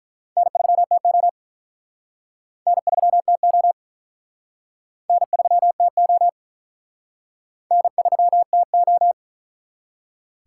Morse code, heard as a keyed tone, about 45 wpm, 700 Hz. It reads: N3TO N3TO N3TO N3TO